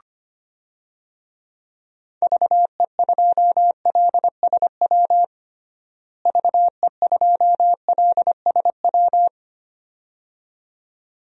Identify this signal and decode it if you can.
Morse code, heard as a keyed tone, about 25 wpm, 700 Hz. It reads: VE2LSW VE2LSW
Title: DXCC-100 Call Signs - 6 Characters - Repeated 2x - 25wpm